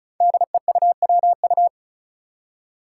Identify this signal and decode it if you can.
Morse code, heard as a keyed tone, about 36 wpm, 700 Hz. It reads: DEUWU
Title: Five Challenging Letter-Number Combinations 35wpm